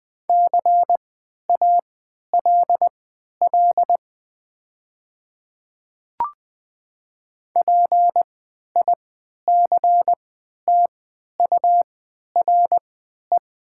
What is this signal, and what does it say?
Morse code, heard as a keyed tone, about 20 wpm, spaced as for 11 wpm, 700 Hz. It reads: CALL PICTURE
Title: 3x Character Spacing - Top 500 Words - 20wpm